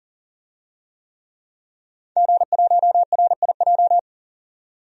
Morse code, heard as a keyed tone, about 40 wpm, 700 Hz. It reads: G1RIJ
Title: DXCC-100 Call Signs - 5 Characters - 40wpm